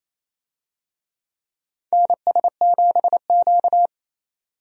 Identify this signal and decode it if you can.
Morse code, heard as a keyed tone, about 28 wpm, 700 Hz. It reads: NS7Q
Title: USA Call Signs - 4 Characters - 28wpm